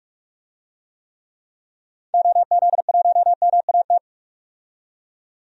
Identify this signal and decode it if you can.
Morse code, heard as a keyed tone, about 46 wpm, 700 Hz. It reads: OZ1MAT
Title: DXCC-100 Call Signs - 6 Characters - 45wpm